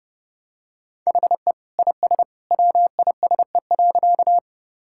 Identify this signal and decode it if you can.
Morse code, heard as a keyed tone, about 30 wpm, 700 Hz. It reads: HE IS WISE.
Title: Sentences from Top 1600 Words 30wpm